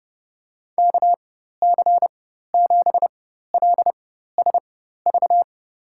Morse code, heard as a keyed tone, about 30 wpm, spaced as for 14 wpm, 700 Hz. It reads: KC7LSV
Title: USA Call Signs - 4x Character Spacing - 30wpm